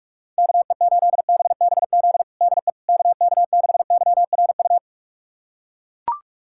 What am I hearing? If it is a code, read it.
Morse code, heard as a keyed tone, about 41 wpm, 700 Hz. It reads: KE8BBZ DE KK6YRU
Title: USA Callsign DE Callsign - 6 Characters - 45wpm